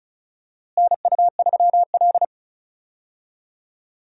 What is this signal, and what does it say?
Morse code, heard as a keyed tone, about 34 wpm, 700 Hz. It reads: NU3L
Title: USA Call Signs - 4 Characters - Repeated 3x - 35wpm